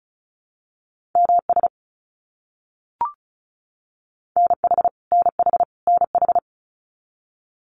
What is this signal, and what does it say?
Morse code, heard as a keyed tone, about 34 wpm, 700 Hz. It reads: MS NH NH NH
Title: US States Repeated 3x 35wpm